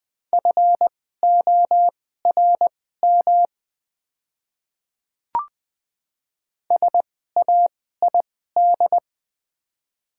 Morse code, heard as a keyed tone, about 20 wpm, spaced as for 14 wpm, 700 Hz. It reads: FORM SAID
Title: 2x Character Spacing - Top 200 Words - 20wpm